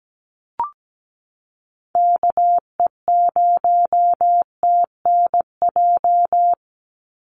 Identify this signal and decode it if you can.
Morse code, heard as a keyed tone, about 17 wpm, 700 Hz. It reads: KE0TNJ
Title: USA Call Signs 17wpm